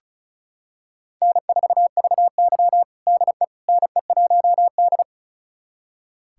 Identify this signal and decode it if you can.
Morse code, heard as a keyed tone, about 34 wpm, 700 Hz. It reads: N4VY DE NE1D
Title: USA Callsign DE Callsign - 4 Characters - 35wpm